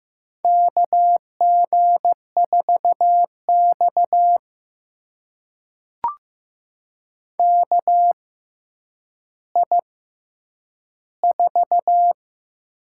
Morse code, heard as a keyed tone, about 15 wpm, spaced as for 9 wpm, 700 Hz. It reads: KG4X KI4
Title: USA Call Signs - 6x Character Spacing - 15wpm